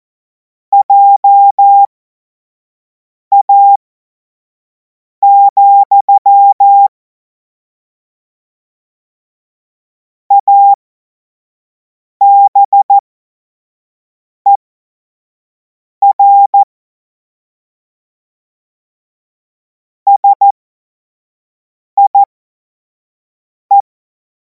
Morse code, heard as a keyed tone, about 14 wpm, spaced as for 5 wpm, 800 Hz. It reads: JA, ABER SIE
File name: Morsekurs / Witze / Witz 17. März 2026 25 F70